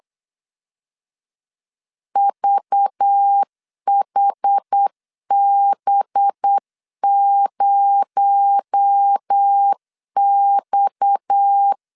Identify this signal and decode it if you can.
Morse code, heard as a keyed tone, about 8 wpm, 800 Hz. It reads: VHB0X
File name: LEZIONI / 6 PKX0 / 6 PKX0 8wpm 3341